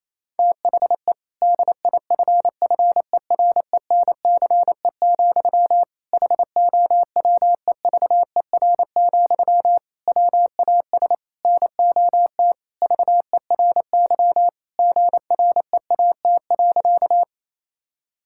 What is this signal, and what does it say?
Morse code, heard as a keyed tone, about 27 wpm, 700 Hz. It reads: THE DIFFERENCE, HOWEVER, WAS NOT VERY GREAT.
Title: Sentences from Top 1000 Words - Review Part 5 - 28wpm